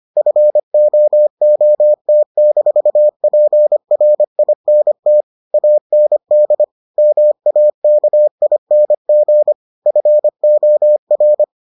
Morse code, heard as a keyed tone, about 25 wpm, 600 Hz. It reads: FOOT-PRINT AND MAKING FOR